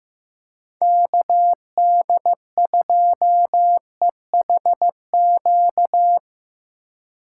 Morse code, heard as a keyed tone, about 16 wpm, 700 Hz. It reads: KD2EHQ
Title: USA Call Signs - 6 Characters - 6x Character Spacing - 15wpm